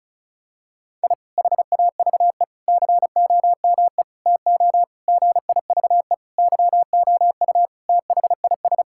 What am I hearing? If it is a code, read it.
Morse code, heard as a keyed tone, about 34 wpm, 700 Hz. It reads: I HAVE COME TO GIVE YOU THIS
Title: Sentences from Top 1000 Words - Review Part 6 - 35wpm